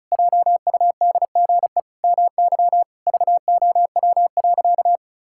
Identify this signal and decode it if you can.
Morse code, heard as a keyed tone, about 34 wpm, 700 Hz. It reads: JUDGE MY VOW.
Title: Pangram Challenge v2 - 35wpm